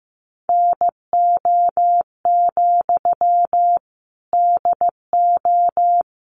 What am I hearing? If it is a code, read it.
Morse code, heard as a keyed tone, about 15 wpm, 700 Hz. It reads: NO, DO
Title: Sentences from Top 1200 Words 15wpm